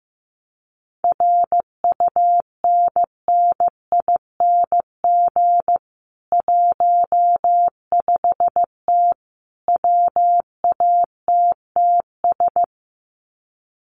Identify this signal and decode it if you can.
Morse code, heard as a keyed tone, about 15 wpm, 700 Hz. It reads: RUNNING 15T WATTS
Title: Common QSO Phrases 15wpm